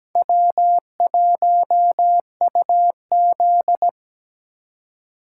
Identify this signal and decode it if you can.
Morse code, heard as a keyed tone, about 20 wpm, 700 Hz. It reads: W1UZ
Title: Speed Racing - USA Call Signs 17wpm